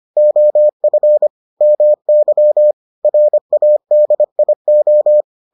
Morse code, heard as a keyed tone, about 25 wpm, 600 Hz. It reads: OF MY RADIO